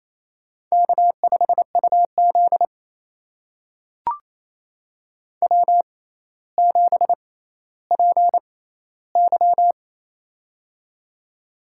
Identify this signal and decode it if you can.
Morse code, heard as a keyed tone, about 28 wpm, spaced as for 14 wpm, 700 Hz. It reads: K5UZ W7PY
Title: USA Call Signs - 4 Characters - 6x Character Spacing - 28wpm